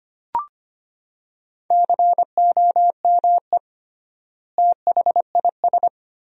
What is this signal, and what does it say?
Morse code, heard as a keyed tone, about 25 wpm, 700 Hz. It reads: COME THIS
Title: 3x Word-Spacing - Sentences from Top 100 Words - 25wpm